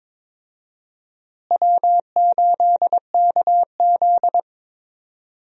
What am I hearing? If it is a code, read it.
Morse code, heard as a keyed tone, about 22 wpm, 700 Hz. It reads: W8KZ
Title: USA Call Signs - 4 Characters - Repeated 2x - 22wpm